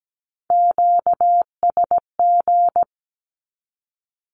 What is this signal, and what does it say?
Morse code, heard as a keyed tone, about 17 wpm, 700 Hz. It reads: QSG
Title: Q-Codes with Spoken Meaning - 17wpm